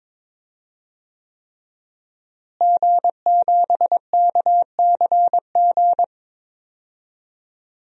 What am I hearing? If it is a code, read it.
Morse code, heard as a keyed tone, about 22 wpm, 700 Hz. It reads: G7KCG
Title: DXCC-100 Call Signs - 5 Characters - Repeated 3x - 22wpm